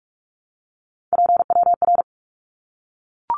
Morse code, heard as a keyed tone, about 44 wpm, 700 Hz. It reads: PWR
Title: Abbreviations 45wpm